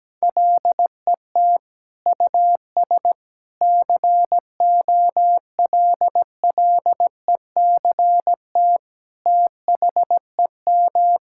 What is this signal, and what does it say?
Morse code, heard as a keyed tone, about 17 wpm, 700 Hz. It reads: LET US COLLECT THEM
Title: Sentences from Top 1600 Words 17wpm